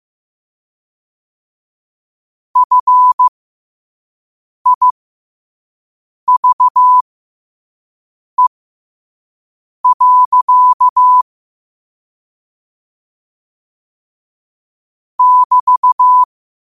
Morse code, heard as a keyed tone, about 15 wpm, spaced as for 5 wpm, 1000 Hz. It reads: FIVE. <BT>
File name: qotdfiles / 5wpm